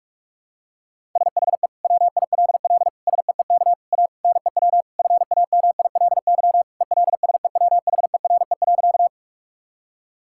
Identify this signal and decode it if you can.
Morse code, heard as a keyed tone, about 46 wpm, 700 Hz. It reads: SHE WILL SEEK A NEW FAMILY ELSEWHERE.
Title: Sentences from Top 1600 Words 45wpm